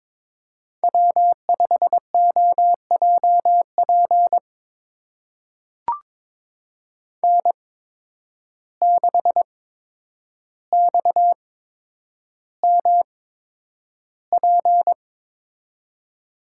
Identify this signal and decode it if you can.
Morse code, heard as a keyed tone, about 22 wpm, spaced as for 9 wpm, 700 Hz. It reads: W5OJP N6XMP
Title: USA Call Signs - 5 Characters - 8x Character Spacing - 22wpm